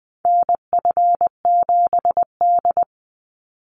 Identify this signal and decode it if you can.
Morse code, heard as a keyed tone, about 20 wpm, 700 Hz. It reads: NF7D
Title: Call Signs - 20wpm